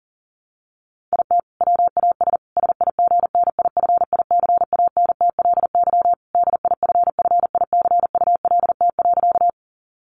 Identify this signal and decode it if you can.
Morse code, heard as a keyed tone, about 40 wpm, 700 Hz. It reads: IT WAS SIGNIFICANTLY DIFFICULT.
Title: Sentences from Top 1500 Words 40wpm